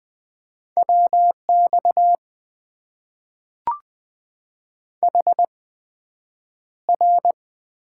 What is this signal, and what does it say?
Morse code, heard as a keyed tone, about 20 wpm, spaced as for 9 wpm, 700 Hz. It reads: WX HR
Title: Top 90 QSO Elements - 8x Character Spacing - 20wpm